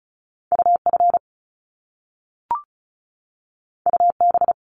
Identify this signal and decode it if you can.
Morse code, heard as a keyed tone, about 36 wpm, 700 Hz. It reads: UF UB
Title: Top Two Letter Combinations in Words 35wpm